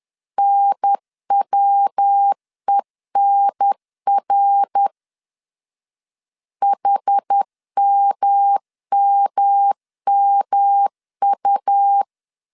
Morse code, heard as a keyed tone, about 10 wpm, 800 Hz. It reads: NWENR HMMMU
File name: CW / DET / 1 3 1500C S B 10wpm 3335